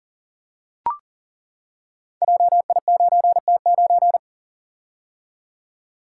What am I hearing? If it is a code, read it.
Morse code, heard as a keyed tone, about 42 wpm, 700 Hz. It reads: JI9T9
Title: Code Groups - 5 Letter:Numbers - 40wpm